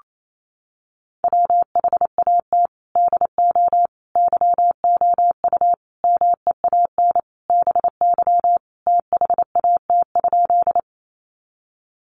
Morse code, heard as a keyed tone, about 28 wpm, 700 Hz. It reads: WHAT DO YOU MEAN BY THAT?
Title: Sentences from Top 200 Words 28wpm